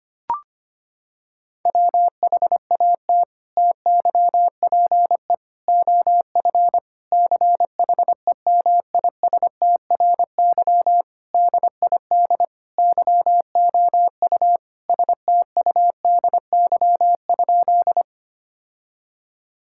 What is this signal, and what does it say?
Morse code, heard as a keyed tone, about 25 wpm, 700 Hz. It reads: WHAT TYPE OF CHEMISTRY DID YOU STUDY?
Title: Sentences from Top 1000 Words - Review Part 6 - 25wpm